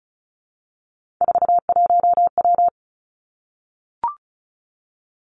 Morse code, heard as a keyed tone, about 36 wpm, 700 Hz. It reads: 41W
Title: Three Challenging Letter-Number Combinations 35wpm